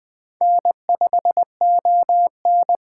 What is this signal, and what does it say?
Morse code, heard as a keyed tone, about 20 wpm, 700 Hz. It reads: N5ON
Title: Speed Racing - USA Call Signs 17wpm